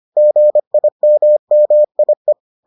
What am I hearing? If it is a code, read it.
Morse code, heard as a keyed tone, about 25 wpm, 600 Hz. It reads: GIMMIE